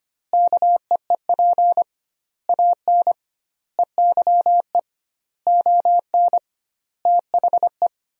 Morse code, heard as a keyed tone, about 25 wpm, 700 Hz. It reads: KEEP AN EYE ON THE
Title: 2x Word-Spacing - Sentences from Top 300 Words - 25wpm